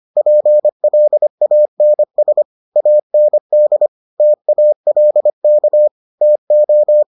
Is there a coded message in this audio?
Yes: Morse code from 0.2 to 7.0 s, heard as a keyed tone, about 25 wpm, 600 Hz. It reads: PLANS AND TALK TO